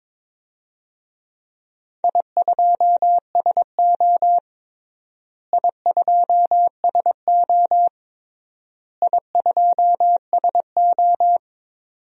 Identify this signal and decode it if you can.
Morse code, heard as a keyed tone, about 22 wpm, 700 Hz. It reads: I2SO I2SO I2SO
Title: DXCC-100 Call Signs - 4 Characters - Repeated 3x - 22wpm